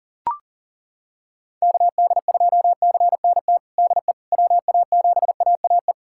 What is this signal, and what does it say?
Morse code, heard as a keyed tone, about 40 wpm, 700 Hz. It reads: KD2CNT DE WA7AAE
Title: USA Callsign DE Callsign - 6 Characters - 40wpm